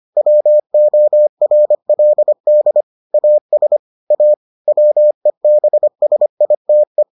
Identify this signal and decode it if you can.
Morse code, heard as a keyed tone, about 25 wpm, 600 Hz. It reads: WORLD AS A WEBSITE